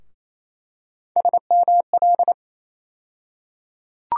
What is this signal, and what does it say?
Morse code, heard as a keyed tone, about 28 wpm, 700 Hz. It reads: SML
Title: USA Call Sign Postfixes - 3 Letters 28wpm